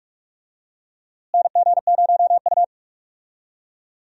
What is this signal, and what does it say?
Morse code, heard as a keyed tone, about 46 wpm, 700 Hz. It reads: NG0U
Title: USA Call Signs - 4 Characters - Repeated 3x - 45wpm